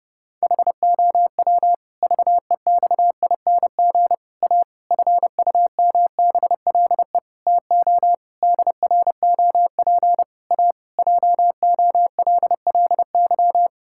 Morse code, heard as a keyed tone, about 28 wpm, 700 Hz. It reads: HOW VEXING A FUMBLE TO DROP A JOLLY
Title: Pangram Challenge v2 - 30wpm